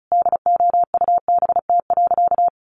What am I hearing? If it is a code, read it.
Morse code, heard as a keyed tone, about 34 wpm, 700 Hz. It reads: DOUBT.
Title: Sentences from Top 1500 Words 35wpm